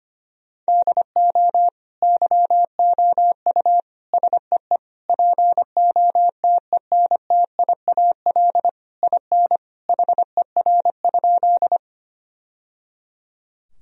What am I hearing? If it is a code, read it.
Morse code, heard as a keyed tone, about 25 wpm, 700 Hz. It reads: DO YOU SEE POTENTIAL IN HER?
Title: Sentences from Top 1000 Words - Review Part 4 - 25wpm